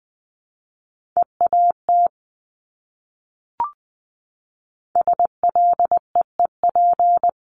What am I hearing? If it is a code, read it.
Morse code, heard as a keyed tone, about 19 wpm, 700 Hz. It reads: EAT SLEEP
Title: Top 500 Words Repeated 20wpm